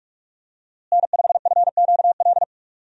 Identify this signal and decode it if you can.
Morse code, heard as a keyed tone, about 46 wpm, 700 Hz. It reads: N5FQL
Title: USA Call Signs - 5 Characters - Repeated 2x - 45wpm